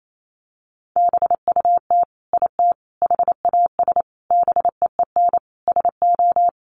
Morse code, heard as a keyed tone, about 28 wpm, 700 Hz. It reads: BUT IT HAS BEEN SO
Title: Sentences from Top 100 Words 28wpm